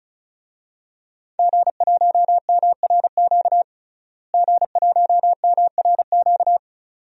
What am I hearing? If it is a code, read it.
Morse code, heard as a keyed tone, about 34 wpm, 700 Hz. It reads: G1MRQ G1MRQ
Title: DXCC-100 Call Signs - 5 Characters - Repeated 2x - 35wpm